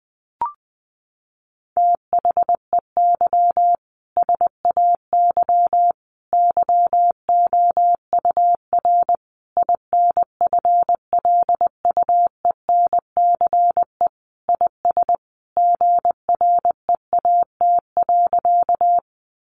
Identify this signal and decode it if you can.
Morse code, heard as a keyed tone, about 20 wpm, 700 Hz. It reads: THEY SAY YOUR INFLUENCE IS GREAT.
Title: Sentences from Top 1100 Words 20wpm